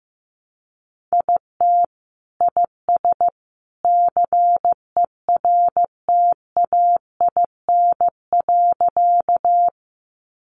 Morse code, heard as a keyed tone, about 15 wpm, 700 Hz. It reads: IT IS CERTAIN.
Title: Sentences from Top 500 Words 15wpm